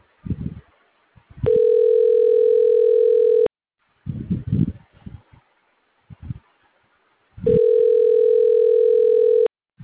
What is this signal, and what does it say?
Ringback tone (North America), 440 + 480 Hz.